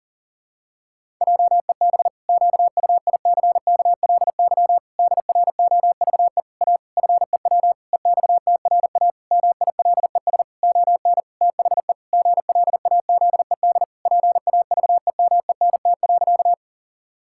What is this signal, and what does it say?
Morse code, heard as a keyed tone, about 40 wpm, 700 Hz. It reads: JEB QUICKLY DROVE A FEW EXTRA MILES ON THE GLAZED PAVEMENT.
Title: Pangram Challenge v2 - 40wpm